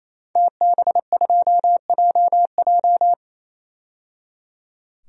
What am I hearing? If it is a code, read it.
Morse code, heard as a keyed tone, about 28 wpm, 700 Hz. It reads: TB2JJ
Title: DXCC-100 Call Signs - 5 Characters - 28wpm